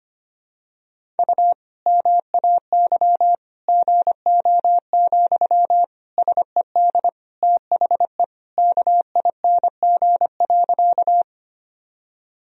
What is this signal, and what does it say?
Morse code, heard as a keyed tone, about 25 wpm, 700 Hz. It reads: U MAY GO, SED THE KING.
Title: Sentences from Top 400 Words and Abbreviations 25wpm